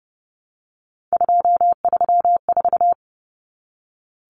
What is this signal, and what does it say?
Morse code, heard as a keyed tone, about 30 wpm, 700 Hz. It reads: 234
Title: Numbers - 3-Digit 30wpm